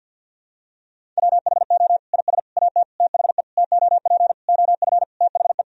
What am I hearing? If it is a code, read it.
Morse code, heard as a keyed tone, about 50 wpm, 700 Hz. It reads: WHO IS AT THE TOP OF THE
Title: Sentences from Top 1000 Words - Review Part 4 - 50wpm